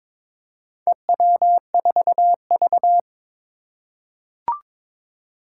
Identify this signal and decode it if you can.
Morse code, heard as a keyed tone, about 22 wpm, 700 Hz. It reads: EW4V
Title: Four Challenging Letter-Number Combinations 22wpm